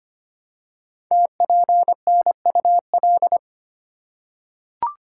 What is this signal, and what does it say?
Morse code, heard as a keyed tone, about 25 wpm, 700 Hz. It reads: TPNUL
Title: Code Groups - 5 Letters - 25wpm